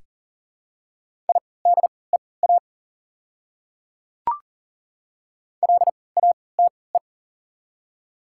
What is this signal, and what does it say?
Morse code, heard as a keyed tone, about 40 wpm, spaced as for 23 wpm, 690 Hz. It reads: IDEA LATE
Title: 3x Character Spacing - Top 300 Words - 40wpm